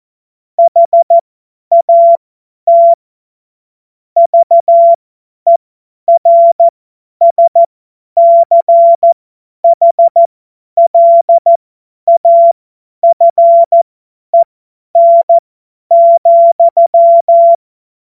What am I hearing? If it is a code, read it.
Morse code, heard as a keyed tone, about 14 wpm, spaced as for 10 wpm, 680 Hz. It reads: HAT VERSCHLAFEN,
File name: Morsekurs / Witze / Witz 14. Februar 2023 50 F70